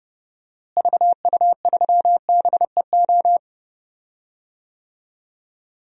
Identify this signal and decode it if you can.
Morse code, heard as a keyed tone, about 32 wpm, 700 Hz. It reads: VU3BEO
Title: DXCC-100 Call Signs - 6 Characters - Repeated 2x - 30wpm